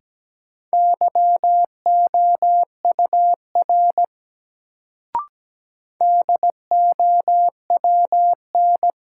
Morse code, heard as a keyed tone, about 17 wpm, 700 Hz. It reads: YOUR DOWN
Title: Sets of 2 Words - New Word Drill - 17wpm